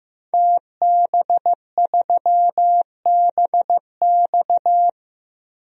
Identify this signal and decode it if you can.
Morse code, heard as a keyed tone, about 15 wpm, 700 Hz. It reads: TB3BX
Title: DXCC-100 Call Signs - 5 Characters - 15wpm